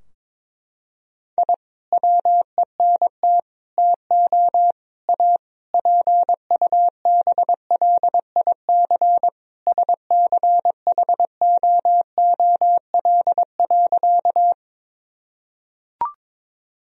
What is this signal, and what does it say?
Morse code, heard as a keyed tone, about 22 wpm, 690 Hz. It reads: I WENT TO A PUBLIC SCHOOL.
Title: Sentences from Top 1000 Words - Review Part 3 - 22wpm